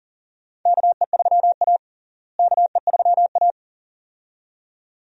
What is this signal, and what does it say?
Morse code, heard as a keyed tone, about 40 wpm, 700 Hz. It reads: KE3A KE3A
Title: USA Call Signs - 4 Characters - Repeated 2x - 40wpm